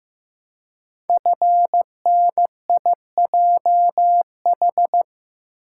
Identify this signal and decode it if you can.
Morse code, heard as a keyed tone, about 15 wpm, 700 Hz. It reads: FNIJH
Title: Code Groups - 5 Letters - 15wpm